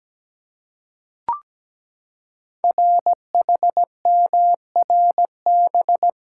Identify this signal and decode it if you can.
Morse code, heard as a keyed tone, about 17 wpm, 700 Hz. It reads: RHMRB
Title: Code Groups - 5 Letters - 17wpm